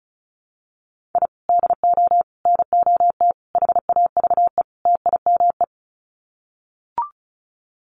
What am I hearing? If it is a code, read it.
Morse code, heard as a keyed tone, about 34 wpm, 700 Hz. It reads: I DO NOT HAVE TIME
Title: Sets of 5 Words Encore 35wpm